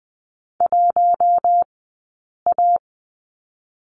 Morse code, heard as a keyed tone, about 20 wpm, 700 Hz. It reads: 1 A
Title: 040 Lesson 8d - Number 1 - Two Characters 20wpm